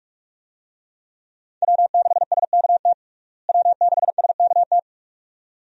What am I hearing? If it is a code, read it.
Morse code, heard as a keyed tone, about 46 wpm, 700 Hz. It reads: W6SKT W6SKT
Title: USA Call Signs - 5 Characters - Repeated 2x - 45wpm